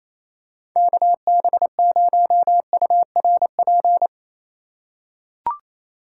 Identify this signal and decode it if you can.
Morse code, heard as a keyed tone, about 28 wpm, 700 Hz. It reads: KB0URP
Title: USA Call Signs - 2x Character Spacing - 28wpm